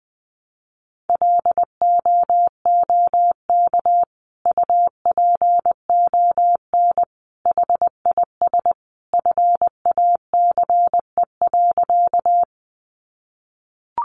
Morse code, heard as a keyed tone, about 20 wpm, 700 Hz. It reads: LOOK UPON HIS FACE.
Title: Sentences from Top 1100 Words 20wpm